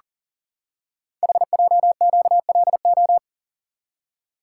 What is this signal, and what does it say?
Morse code, heard as a keyed tone, about 40 wpm, 700 Hz. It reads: HJQLO
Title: Code Groups - 5 Letters - 40wpm